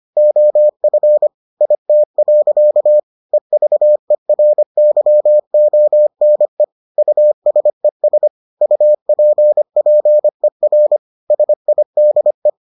Morse code, heard as a keyed tone, about 25 wpm, 600 Hz. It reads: OF IT. EVERYONE USES UPPER SIDE